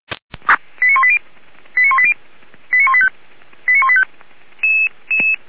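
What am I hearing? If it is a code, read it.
Five-tone selective call (ZVEI), heard as quick runs of five tones, tones of about 70 ms. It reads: 89180 89180 89185 89185
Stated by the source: ZVEI five-tone selective call received by SDR, archived by Feuerwehr Orlamünde (fire brigade).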